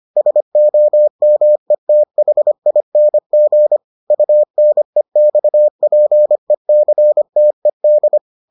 Morse code, heard as a keyed tone, about 25 wpm, 600 Hz. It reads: SOMETHING UNEXPECTED